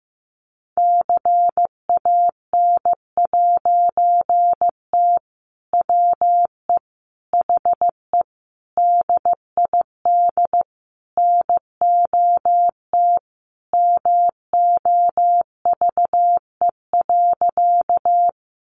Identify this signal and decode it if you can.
Morse code, heard as a keyed tone, about 15 wpm, 700 Hz. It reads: CAN'T WE HE DID NOT MOVE.
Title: A Sound Of Thunder 15wpm